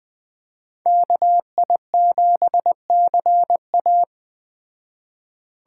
Morse code, heard as a keyed tone, about 21 wpm, 700 Hz. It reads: KI7CA
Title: Speed Racing - USA Call Signs 17wpm